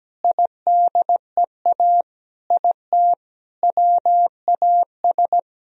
Morse code, heard as a keyed tone, about 17 wpm, 700 Hz. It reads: IDEA IT WAS
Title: Sentences from Top 1000 Words - Review Part 1 - 17wpm